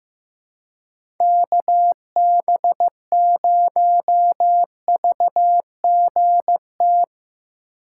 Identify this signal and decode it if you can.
Morse code, heard as a keyed tone, about 15 wpm, 700 Hz. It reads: KB0VGT
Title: USA Call Signs - 3x Character Spacing - 15wpm